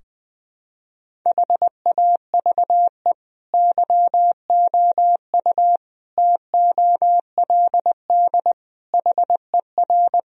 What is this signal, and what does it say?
Morse code, heard as a keyed tone, about 20 wpm, 700 Hz. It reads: HAVE YOU TOLD HER
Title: Sentences from Top 1000 Words - Review Part 2 - 20wpm